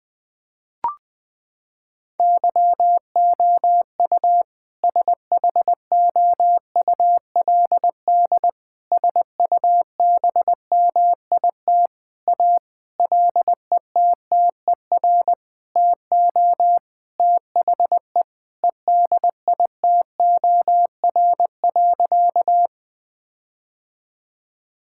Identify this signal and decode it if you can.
Morse code, heard as a keyed tone, about 20 wpm, 700 Hz. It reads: YOU SHOULD SUBMIT A LETTER TO THE EDITOR.
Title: Sentences with 6-Word Phrases - 20wpm